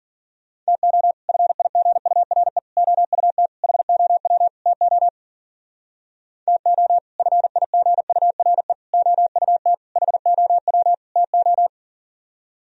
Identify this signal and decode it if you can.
Morse code, heard as a keyed tone, about 51 wpm, 700 Hz. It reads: TO FIGURE OUT HOW TO TO FIGURE OUT HOW TO